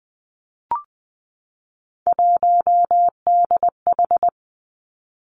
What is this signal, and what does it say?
Morse code, heard as a keyed tone, about 19 wpm, 700 Hz. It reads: 1DH
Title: Three Challenging Letter-Number Combinations 20wpm